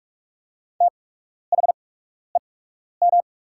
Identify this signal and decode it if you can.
Morse code, heard as a keyed tone, about 44 wpm, spaced as for 12 wpm, 690 Hz. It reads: THEM
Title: 8x Character Spacing - Top 500 Words - 45wpm